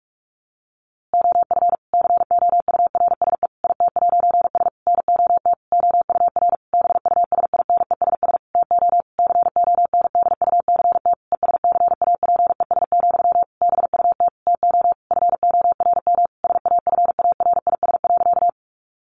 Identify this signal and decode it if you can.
Morse code, heard as a keyed tone, about 44 wpm, 700 Hz. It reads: OF COURSE IT'S NOT OUR BUSINESS TO CONDUCT ESCAPES, BUT TO FORM SAFARIS.
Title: A Sound Of Thunder 45wpm